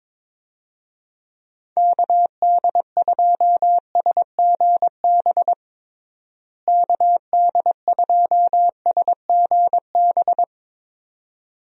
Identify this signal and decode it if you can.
Morse code, heard as a keyed tone, about 22 wpm, 700 Hz. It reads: KD2SGB KD2SGB
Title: USA Call Signs - 6 Characters - Repeated 2x - 22wpm